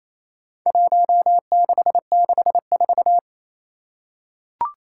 Morse code, heard as a keyed tone, about 27 wpm, 700 Hz. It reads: 1664
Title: Four Challenging Letter-Number Combinations 28wpm